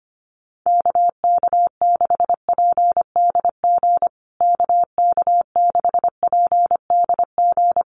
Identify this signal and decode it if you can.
Morse code, heard as a keyed tone, about 25 wpm, 700 Hz. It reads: KK6PDG KK6PDG
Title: USA Call Signs Repeated 3x 25wpm